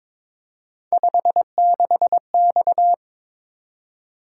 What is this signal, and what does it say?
Morse code, heard as a keyed tone, about 22 wpm, 700 Hz. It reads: 56X
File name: practice / 22wpm / Three Letter-Number Combinations - 22wpm